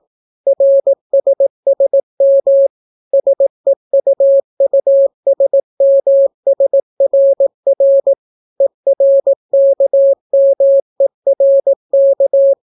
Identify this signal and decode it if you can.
Morse code, heard as a keyed tone, about 18 wpm, 550 Hz. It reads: RSSM SEUUSMSRR ERKMERK